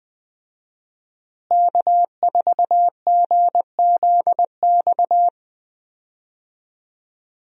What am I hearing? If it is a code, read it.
Morse code, heard as a keyed tone, about 20 wpm, 700 Hz. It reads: K4GZX
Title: USA Call Signs - 5 Characters - Repeated 2x - 20wpm